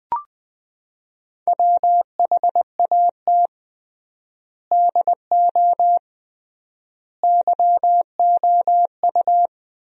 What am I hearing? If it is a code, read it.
Morse code, heard as a keyed tone, about 20 wpm, 700 Hz. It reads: WHAT DO YOU
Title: 3x Word-Spacing - Sentences from Top 100 Words - 20wpm